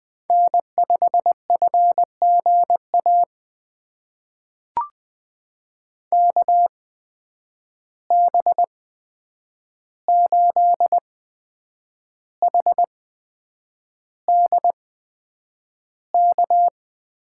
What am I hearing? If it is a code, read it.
Morse code, heard as a keyed tone, about 20 wpm, spaced as for 8 wpm, 700 Hz. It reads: N5FGA KB8HDK
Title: USA Call Signs - 8x Character Spacing - 20wpm